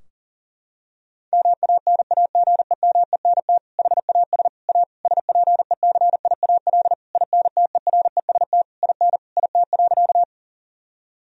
Morse code, heard as a keyed tone, about 40 wpm, 700 Hz. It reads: MANAGEMENT HAS A SPECIAL INTEREST IN IT.
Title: Sentences from Top 1000 Words - Review Part 4 - 40wpm